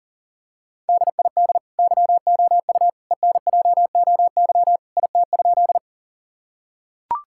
Morse code, heard as a keyed tone, about 37 wpm, 700 Hz. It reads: DID YOU ENJOY IT?
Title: Sentences from Top 1000 Words - Review Part 5 - 40wpm